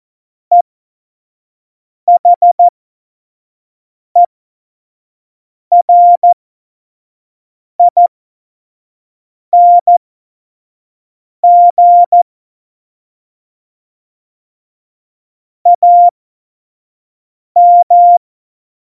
Morse code, heard as a keyed tone, about 14 wpm, spaced as for 5 wpm, 700 Hz. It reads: EHERING AM
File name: Morsekurs / Witze / Witz 25. Juni 2019, 25 F70